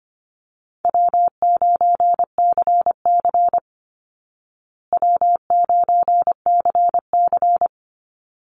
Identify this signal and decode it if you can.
Morse code, heard as a keyed tone, about 25 wpm, 700 Hz. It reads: W9CC W9CC
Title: Easy Call Signs Repeated 2x - 25wpm